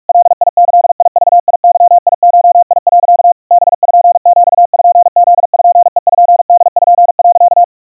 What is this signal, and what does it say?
Morse code (Japanese Wabun code), heard as a keyed tone, about 47 wpm, 700 Hz. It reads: ガギグゲゴ、 パピプペポ、
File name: iroha 150 cpm 700 Hz Crispy